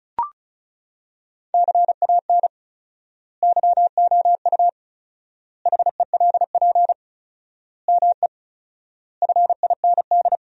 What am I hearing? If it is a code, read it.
Morse code, heard as a keyed tone, about 36 wpm, 700 Hz. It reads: CAN YOU HELP ME FIND
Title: 4x Word-Spacing - Sentences from Top 200 Words - 35wpm